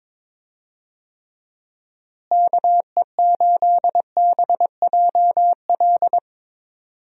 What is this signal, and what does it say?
Morse code, heard as a keyed tone, about 22 wpm, 700 Hz. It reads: KE8BJL
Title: USA Call Signs - 6 Characters - Repeated 2x - 22wpm